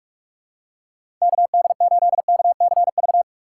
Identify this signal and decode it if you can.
Morse code, heard as a keyed tone, about 46 wpm, 700 Hz. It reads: KD8KCV
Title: USA Call Signs - 4x Character Spacing - 45wpm